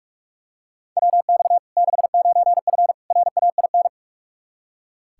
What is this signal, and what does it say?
Morse code, heard as a keyed tone, about 46 wpm, 700 Hz. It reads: WX 69F RAIN
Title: Practice QSO - Exchange 2 - Info - 45wpm